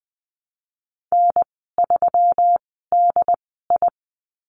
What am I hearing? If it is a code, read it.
Morse code, heard as a keyed tone, about 19 wpm, spaced as for 14 wpm, 700 Hz. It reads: N3DI
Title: 140 Lesson 23g - Letter f - Call Signs 20wpm